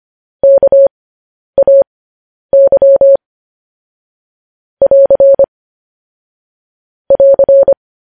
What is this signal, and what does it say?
Morse code, heard as a keyed tone, about 25 wpm, spaced as for 10 wpm, 550 Hz. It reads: KAY <AR> <AR>